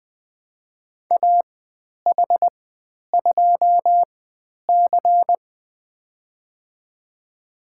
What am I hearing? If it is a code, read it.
Morse code, heard as a keyed tone, about 20 wpm, spaced as for 10 wpm, 700 Hz. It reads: AH2C
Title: 102 Lesson 17g - Letter u - Call Signs 20wpm